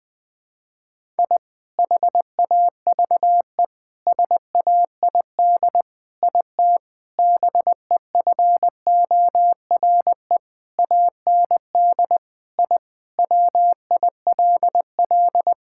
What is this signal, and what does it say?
Morse code, heard as a keyed tone, about 20 wpm, 700 Hz. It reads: I HAVE SAID IT BEFORE AND I WILL
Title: Sentences with 8-Word Phrases - 20wpm